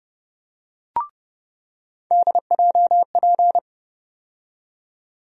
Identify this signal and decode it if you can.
Morse code, heard as a keyed tone, about 30 wpm, 700 Hz. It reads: DJP
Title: USA Call Sign Postfixes - 3 Letters 30wpm